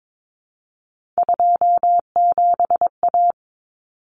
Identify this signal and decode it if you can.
Morse code, heard as a keyed tone, about 22 wpm, 700 Hz. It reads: 27A
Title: Common Cut Numbers - 3-Digit 22wpm